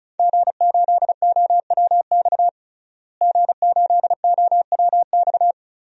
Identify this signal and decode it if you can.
Morse code, heard as a keyed tone, about 34 wpm, 700 Hz. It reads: G8OWX G8OWX
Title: DXCC-100 Call Signs - 5 Characters - Repeated 3x - 35wpm